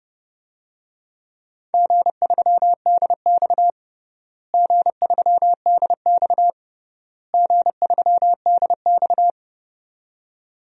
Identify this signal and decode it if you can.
Morse code, heard as a keyed tone, about 30 wpm, 700 Hz. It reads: G3DX G3DX G3DX
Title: DXCC-100 Call Signs - 4 Characters - Repeated 3x - 30wpm